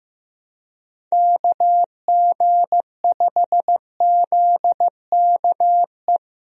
Morse code, heard as a keyed tone, about 15 wpm, 700 Hz. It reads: KG5ZKE
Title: USA Call Signs - 6 Characters - Repeated 2x - 15wpm